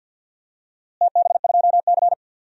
Morse code, heard as a keyed tone, about 48 wpm, 700 Hz. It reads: TB2C
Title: DXCC-100 Call Signs - 4 Characters - Repeated 2x - 50wpm